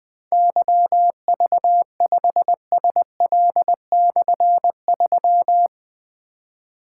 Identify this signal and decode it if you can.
Morse code, heard as a keyed tone, about 20 wpm, 700 Hz. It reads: YV5SL/3
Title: Difficult Call Signs 20wpm